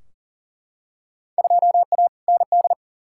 Morse code, heard as a keyed tone, about 40 wpm, 700 Hz. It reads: 2A ND
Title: Field Day Exchanges Encore - 40wpm